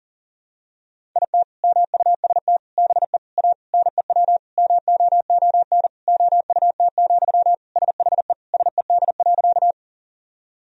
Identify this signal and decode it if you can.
Morse code, heard as a keyed tone, about 40 wpm, 700 Hz. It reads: IT MUST BE A NEW MOON OUT, SHE SED.
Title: Sentences from Top 500 Words and Abbreviations 40wpm